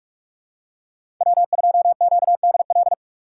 Morse code, heard as a keyed tone, about 44 wpm, 690 Hz. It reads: W2QDL
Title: USA Call Signs - 5 Characters - 8x Character Spacing - 45wpm